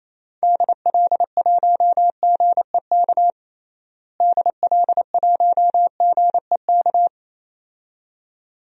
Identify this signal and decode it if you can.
Morse code, heard as a keyed tone, about 28 wpm, 700 Hz. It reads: DL1GEK DL1GEK
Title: DXCC-100 Call Signs - 6 Characters - Repeated 2x - 28wpm